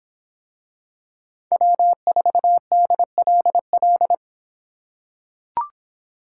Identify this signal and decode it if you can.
Morse code, heard as a keyed tone, about 26 wpm, 700 Hz. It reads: W4DLL